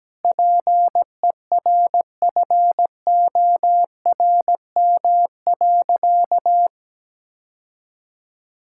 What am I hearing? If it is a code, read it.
Morse code, heard as a keyed tone, about 17 wpm, 700 Hz. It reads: PERFORM.
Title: Sentences from Top 1000 Words - Review Part 5 - 17wpm